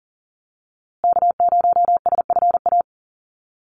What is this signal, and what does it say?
Morse code, heard as a keyed tone, about 40 wpm, 700 Hz. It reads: K0SFA
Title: USA Call Signs Repeated 2x 40wpm